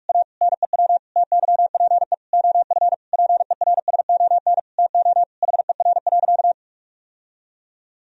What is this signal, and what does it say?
Morse code, heard as a keyed tone, about 46 wpm, 700 Hz. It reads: A NEW TYPE OF PERSON TO HER.
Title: Sentences from Top 1000 Words - Review Part 3 - 45wpm